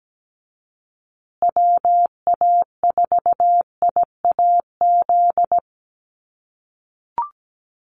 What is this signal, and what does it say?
Morse code, heard as a keyed tone, about 17 wpm, 700 Hz. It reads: WA4IAZ
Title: USA Call Signs Repeated 3x 17wpm